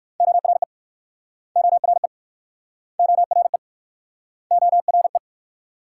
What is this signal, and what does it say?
Morse code, heard as a keyed tone, about 58 wpm, 700 Hz. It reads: ORE ORE ORE ORE